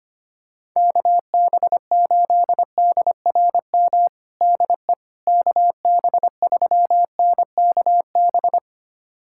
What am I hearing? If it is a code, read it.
Morse code, heard as a keyed tone, about 25 wpm, 700 Hz. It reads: KB8DRM DE KB3NKB
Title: USA Callsign DE Callsign - 6 Characters - 25wpm